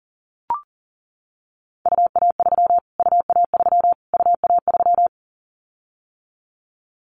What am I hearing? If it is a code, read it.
Morse code, heard as a keyed tone, about 37 wpm, 700 Hz. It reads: UA3 UA3 UA3
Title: Callsign Prefixes for Top 25 DXCC - Repeated 3x - 40wpm